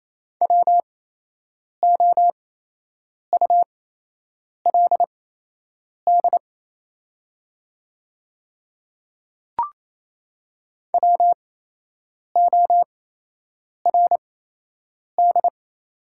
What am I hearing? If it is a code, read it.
Morse code, heard as a keyed tone, about 28 wpm, spaced as for 8 wpm, 690 Hz. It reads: WOULD WORD
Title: 8x Character Spacing - Top 300 Words - 28wpm